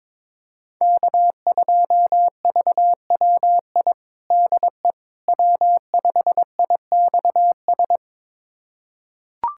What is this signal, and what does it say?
Morse code, heard as a keyed tone, about 22 wpm, 700 Hz. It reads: K2VWI DE W5IXS
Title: USA Callsign DE Callsign - 5 Characters - 22wpm